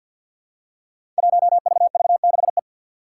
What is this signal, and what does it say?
Morse code, heard as a keyed tone, about 51 wpm, 700 Hz. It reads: 1VV6E
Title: Five Challenging Letter-Number Combinations 50wpm